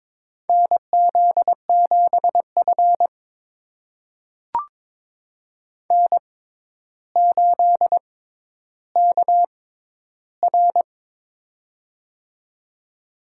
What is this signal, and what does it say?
Morse code, heard as a keyed tone, about 22 wpm, spaced as for 11 wpm, 700 Hz. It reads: NZ7F N8KR
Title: USA Call Signs - 4 Characters - 6x Character Spacing - 22wpm